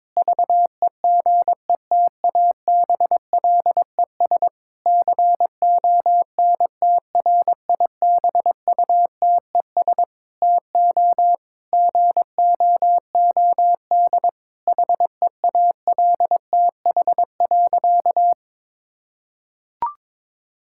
Morse code, heard as a keyed tone, about 22 wpm, 700 Hz. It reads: VEGETABLES CONTRIBUTES TO GOOD HEALTH.
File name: Difficult Sentences with Hints - 22wpm